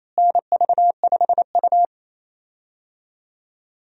Morse code, heard as a keyed tone, about 28 wpm, 700 Hz. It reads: NV5U
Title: USA Call Signs - 4 Characters - Repeated 3x - 28wpm